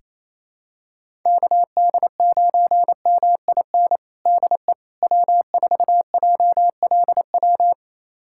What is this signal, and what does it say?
Morse code, heard as a keyed tone, about 28 wpm, 700 Hz. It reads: KD9MIN DE W4JLW
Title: USA Callsign DE Callsign 28wpm